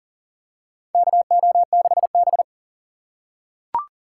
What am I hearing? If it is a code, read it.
Morse code, heard as a keyed tone, about 38 wpm, 700 Hz. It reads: KO6B
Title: USA Call Signs - 3x Character Spacing - 40wpm